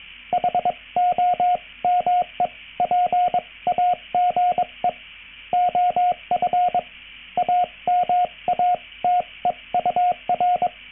Morse code, heard as a keyed tone, about 22 wpm, spaced as for 19 wpm, 720 Hz, with radio noise. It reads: HOMEPAGE OF AMATEUR